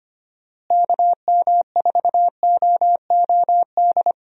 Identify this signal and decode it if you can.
Morse code, heard as a keyed tone, about 25 wpm, 700 Hz. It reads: KM4OOD
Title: USA Call Signs - 6 Characters - Repeated 3x - 25wpm